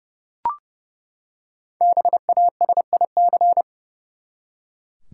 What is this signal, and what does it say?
Morse code, heard as a keyed tone, about 30 wpm, 700 Hz. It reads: BASIC
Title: Five Letter Words - 30wpm